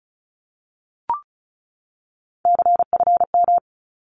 Morse code, heard as a keyed tone, about 34 wpm, 700 Hz. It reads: CFM
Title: Abbreviations 35wpm